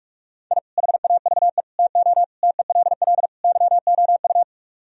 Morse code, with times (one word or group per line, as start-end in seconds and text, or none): 0.5-0.6 I
0.8-1.6 HAVE
1.8-2.2 TO
2.4-3.3 TELL
3.4-4.4 YOU